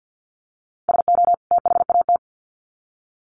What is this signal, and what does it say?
Morse code, heard as a keyed tone, about 50 wpm, 700 Hz. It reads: SO THAT
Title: Sets of 2 Words 50wpm